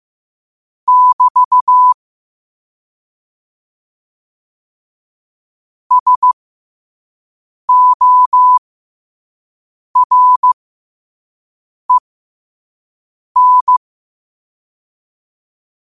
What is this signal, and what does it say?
Morse code, heard as a keyed tone, about 15 wpm, spaced as for 5 wpm, 1000 Hz. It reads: <BT> SOREN
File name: qotdfiles / 5wpm